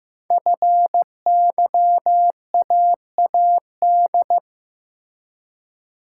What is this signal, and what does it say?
Morse code, heard as a keyed tone, about 15 wpm, 700 Hz. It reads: FYAAD
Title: Code Groups - 5 Letters - 15wpm